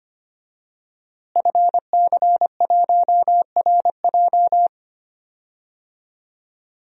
Morse code, heard as a keyed tone, about 25 wpm, 700 Hz. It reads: FC1RJ
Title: DXCC-100 Call Signs - 5 Characters - Repeated 2x - 25wpm